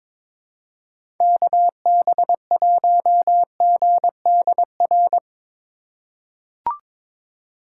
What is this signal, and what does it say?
Morse code, heard as a keyed tone, about 22 wpm, 700 Hz. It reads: KB1GDR
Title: USA Call Signs - 6 Characters - 2x Character Spacing - 22wpm